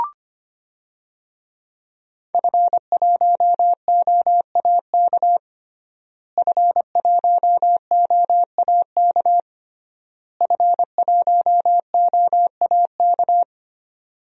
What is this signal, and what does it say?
Morse code, heard as a keyed tone, about 25 wpm, 700 Hz. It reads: F1OAK F1OAK F1OAK
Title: DXCC-100 Call Signs - 5 Characters - Repeated 3x - 25wpm